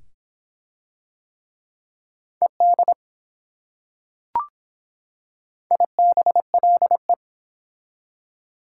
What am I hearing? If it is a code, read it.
Morse code, heard as a keyed tone, about 24 wpm, 700 Hz. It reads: ED IBLE